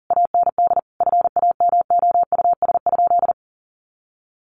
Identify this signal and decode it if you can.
Morse code, heard as a keyed tone, about 40 wpm, 700 Hz. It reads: AND FAMOUS?
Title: Sentences from Top 1500 Words 40wpm